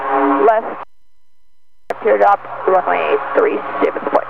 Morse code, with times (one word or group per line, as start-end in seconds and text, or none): none